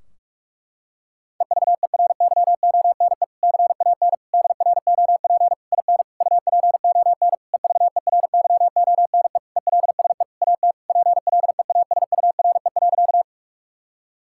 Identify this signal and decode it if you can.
Morse code, heard as a keyed tone, about 46 wpm, 700 Hz. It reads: EVERYONE CAN DROP IN UPON EVERYONE ELSE AT PLEASURE.
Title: Sentences from Top 1600 Words 45wpm